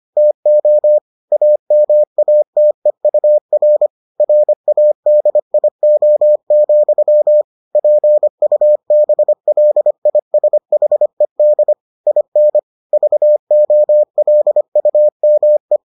Morse code, heard as a keyed tone, about 25 wpm, 600 Hz. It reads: TO AMATEUR RADIO, PUBLISHED IN VOLUME